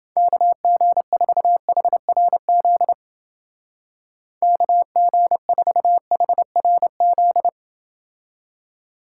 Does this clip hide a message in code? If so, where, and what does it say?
Yes: Morse code from 0.2 to 7.5 s, heard as a keyed tone, about 29 wpm, 700 Hz. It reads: KG4HRZ KG4HRZ